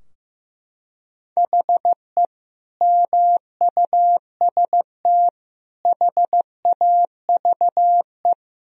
Morse code, heard as a keyed tone, about 16 wpm, 700 Hz. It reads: HE MUST HAVE
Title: Sentences from Top 1000 Words - Review Part 3 - 15wpm